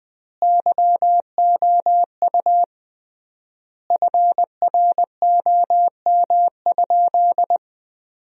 Morse code, heard as a keyed tone, about 20 wpm, 700 Hz. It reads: YOU FROM?
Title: 3x Word-Spacing - Sentences from Top 200 Words - 20wpm